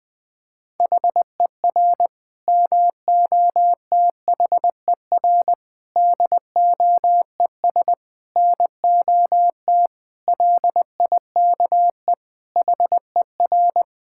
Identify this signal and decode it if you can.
Morse code, heard as a keyed tone, about 20 wpm, 700 Hz. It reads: HER MOTHER DOES NOT LIKE HER
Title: Sentences from Top 1000 Words - Review Part 5 - 20wpm